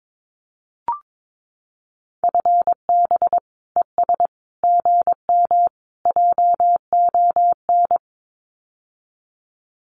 Morse code, heard as a keyed tone, about 22 wpm, 700 Hz. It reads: FB ES GM JON
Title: Common QSO Phrases Encore 22wpm